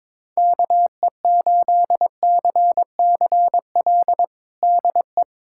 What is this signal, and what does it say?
Morse code, heard as a keyed tone, about 21 wpm, 700 Hz. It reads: KE8CCL DE
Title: USA Callsign DE Callsign - 6 Characters - 22wpm